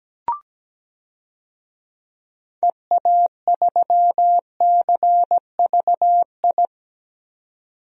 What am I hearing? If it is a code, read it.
Morse code, heard as a keyed tone, about 17 wpm, 700 Hz. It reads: EA3CVI
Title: DXCC-100 Call Signs - 6 Characters - 17wpm